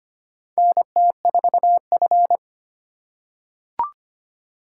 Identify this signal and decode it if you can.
Morse code, heard as a keyed tone, about 25 wpm, 700 Hz. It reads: NT4F
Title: USA Call Signs - 4 Characters - 4x Character Spacing - 25wpm